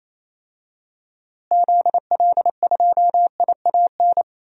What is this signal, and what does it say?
Morse code, heard as a keyed tone, about 28 wpm, 700 Hz. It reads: ZL2IAN
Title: DXCC-100 Call Signs - 6 Characters - Repeated 3x - 28wpm